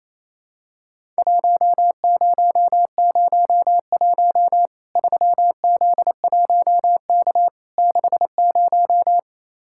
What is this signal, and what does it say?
Morse code, heard as a keyed tone, about 27 wpm, 700 Hz. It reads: 1001 3Z1K 60
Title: Worked All Europe DX Contest - QTC Exchange - 28wpm